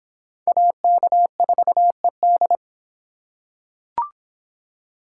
Morse code, heard as a keyed tone, about 25 wpm, 700 Hz. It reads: AK4ED